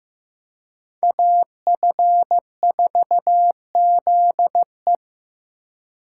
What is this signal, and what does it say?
Morse code, heard as a keyed tone, about 18 wpm, 700 Hz. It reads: AF4ZE
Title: Speed Racing - USA 2x2 Call Signs 15wpm